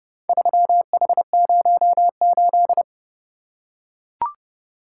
Morse code, heard as a keyed tone, about 28 wpm, 700 Hz. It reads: 3H08